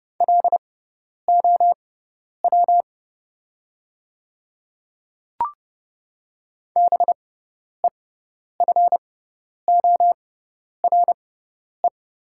Morse code, heard as a keyed tone, about 30 wpm, spaced as for 10 wpm, 690 Hz. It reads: LOW BEFORE
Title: 6x Character Spacing - Top 200 Words - 30wpm